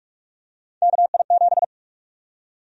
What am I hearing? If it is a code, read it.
Morse code, heard as a keyed tone, about 45 wpm, 700 Hz. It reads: KI7
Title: USA Call Sign Prefixes - Repeated 2x - 45wpm